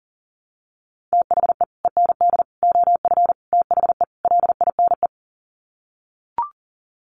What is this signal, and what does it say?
Morse code, heard as a keyed tone, about 40 wpm, 700 Hz. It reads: THE END OF THE LINE
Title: Sets of 5 Words Encore2 40wpm